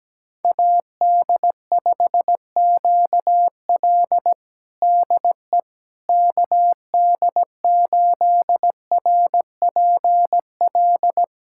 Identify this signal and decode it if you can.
Morse code, heard as a keyed tone, about 17 wpm, 700 Hz. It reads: AD5QL DE KD8RPL
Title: USA Callsign DE Callsign 17wpm